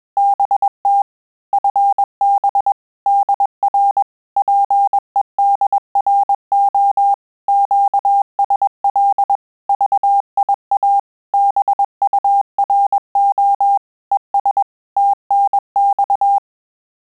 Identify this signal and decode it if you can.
Morse code, heard as a keyed tone, about 21 wpm, 800 Hz. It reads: BT FB DR PEDRO QSL VIA BURO ES TNX
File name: QSO CW 100 CPM